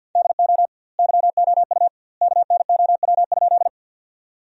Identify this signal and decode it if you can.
Morse code, heard as a keyed tone, about 46 wpm, 700 Hz. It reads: DO YOU KNOW?
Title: 2x Word-Spacing - Sentences from Top 100 Words - 50wpm